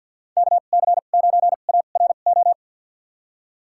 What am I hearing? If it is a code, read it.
Morse code, heard as a keyed tone, about 52 wpm, spaced as for 36 wpm, 700 Hz. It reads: KC9ARO
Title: USA Call Signs - 2x Character Spacing - 50wpm